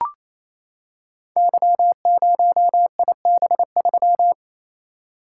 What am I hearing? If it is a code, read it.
Morse code, heard as a keyed tone, about 29 wpm, 700 Hz. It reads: Y0IB3
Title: Code Groups - 5 Letter:Numbers - 28wpm